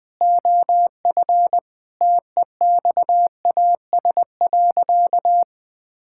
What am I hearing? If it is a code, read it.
Morse code, heard as a keyed tone, about 20 wpm, 700 Hz. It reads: OF TEXAS.
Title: Sentences with 4-Word Phrases - Spoken Beginning - 20wpm